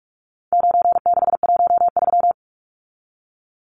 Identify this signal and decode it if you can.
Morse code, heard as a keyed tone, about 44 wpm, 700 Hz. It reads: 9613
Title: Numbers - 4-Digit 45wpm